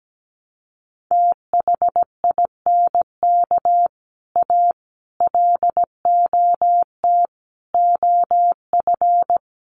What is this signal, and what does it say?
Morse code, heard as a keyed tone, about 17 wpm, 700 Hz. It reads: THINK A LOT OF
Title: Sets of 5 Words 17wpm